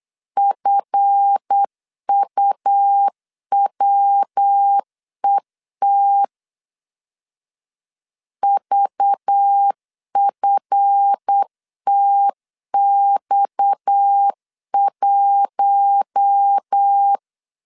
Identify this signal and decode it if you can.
Morse code, heard as a keyed tone, about 8 wpm, 800 Hz. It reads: FUWET VFTX1
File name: LEZIONI / 8 YJ45 / 8 YJ45 8wpm 3214